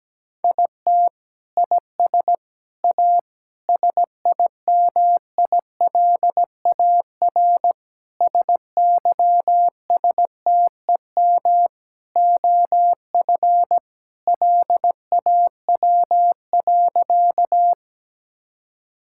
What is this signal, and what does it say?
Morse code, heard as a keyed tone, about 17 wpm, 700 Hz. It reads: IT IS A SIMILAR SYSTEM OF LAW.
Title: Sentences from Top 1000 Words - Review Part 5 - 17wpm